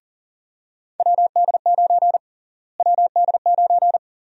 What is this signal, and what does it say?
Morse code, heard as a keyed tone, about 40 wpm, 700 Hz. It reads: WD9 WD9
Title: USA Call Sign Prefixes - Repeated 2x - 40wpm